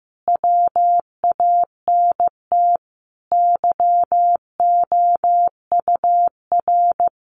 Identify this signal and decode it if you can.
Morse code, heard as a keyed tone, about 15 wpm, 700 Hz. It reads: WANT YOUR
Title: Sentences from Top 1300 Words 15wpm